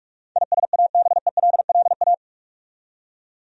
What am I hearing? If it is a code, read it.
Morse code, heard as a keyed tone, about 46 wpm, 700 Hz. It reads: ISABELLA
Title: USA Common Names - 45wpm